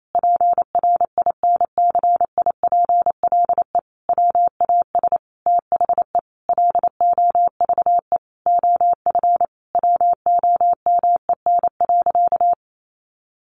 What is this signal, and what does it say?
Morse code, heard as a keyed tone, about 28 wpm, 700 Hz. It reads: PRINCIPLE WAS THE LOVE OF WOMEN.
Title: Sentences from Top 1400 Words 28wpm